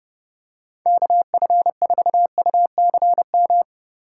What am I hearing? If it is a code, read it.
Morse code, heard as a keyed tone, about 30 wpm, 700 Hz. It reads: KF4UCM
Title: USA Call Signs - 6 Characters - 3x Character Spacing - 30wpm